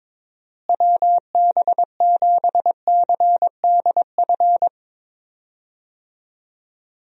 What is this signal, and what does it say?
Morse code, heard as a keyed tone, about 21 wpm, 700 Hz. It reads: WB7CDF
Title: USA Call Signs - 6 Characters - Repeated 3x - 22wpm